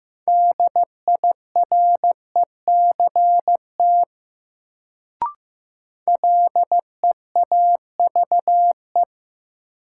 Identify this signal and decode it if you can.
Morse code, heard as a keyed tone, about 15 wpm, 700 Hz. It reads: DIRECT LEAVE
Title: Sentences from Top 400 Words - New Word Drill - 15wpm